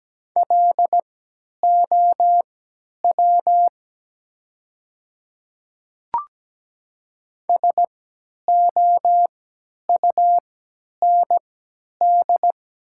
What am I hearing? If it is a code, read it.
Morse code, heard as a keyed tone, about 17 wpm, spaced as for 10 wpm, 700 Hz. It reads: LOW SOUND
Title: 3x Character Spacing - Top 200 Words - 17wpm